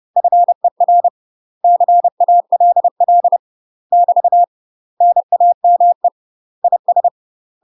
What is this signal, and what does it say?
Morse code, heard as a keyed tone, about 30 wpm, 700 Hz. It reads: FER CALL <BT> NAME IS
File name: UK MASTER QSOs TRAINING / 30WPM / UK MASTER QSOs 0033